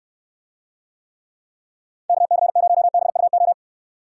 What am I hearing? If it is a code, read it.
Morse code, heard as a keyed tone, about 69 wpm, 700 Hz. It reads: KC0BUO